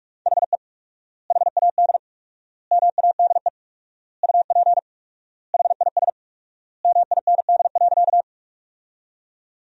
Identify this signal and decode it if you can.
Morse code, heard as a keyed tone, about 45 wpm, 700 Hz. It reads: HE HAD MADE UP HIS MIND.
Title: 4x Word-Spacing - Sentences from Top 500 Words - 45wpm